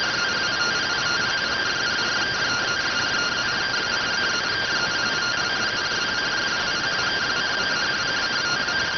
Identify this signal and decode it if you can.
Two-tone teleprinter signal, about 170 Hz shift.